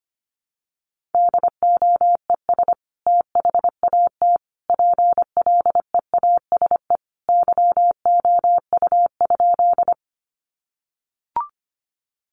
Morse code, heard as a keyed tone, about 25 wpm, 700 Hz. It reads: DOES THAT PLEASE YOU?
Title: Sentences from Top 1500 Words 25wpm